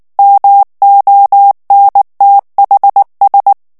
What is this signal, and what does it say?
Morse code, heard as a keyed tone, about 19 wpm, 800 Hz. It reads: MONTHS
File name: ANGLAIS texte clair 3 vitesse 18 mots minute